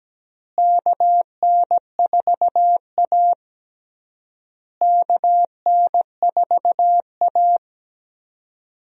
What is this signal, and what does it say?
Morse code, heard as a keyed tone, about 17 wpm, 700 Hz. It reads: KN4A KN4A
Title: USA Call Signs - 4 Characters - Repeated 3x - 17wpm